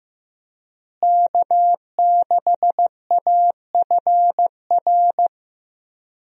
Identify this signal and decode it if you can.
Morse code, heard as a keyed tone, about 16 wpm, 700 Hz. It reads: K6AFR
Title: USA Call Signs - 5 Characters - Repeated 3x - 15wpm